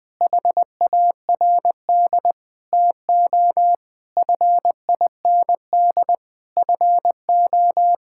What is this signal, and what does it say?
Morse code, heard as a keyed tone, about 21 wpm, 700 Hz. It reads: HARD TO FIND FO
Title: Sentences with 3-Word Phrases - 20wpm